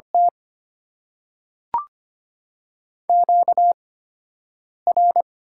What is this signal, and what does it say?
Morse code, heard as a keyed tone, about 25 wpm, spaced as for 7 wpm, 700 Hz. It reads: T QR
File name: Top 90 QSO Elements - 8x Character Spacing - 25wpm